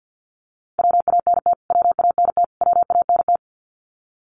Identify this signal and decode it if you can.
Morse code, heard as a keyed tone, about 49 wpm, 700 Hz. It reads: WANT WANT WANT
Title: Top 200 Words Repeated 50wpm